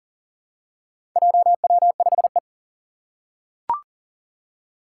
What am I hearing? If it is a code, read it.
Morse code, heard as a keyed tone, about 40 wpm, 700 Hz. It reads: JW5E
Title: Four Challenging Letter-Number Combinations 40wpm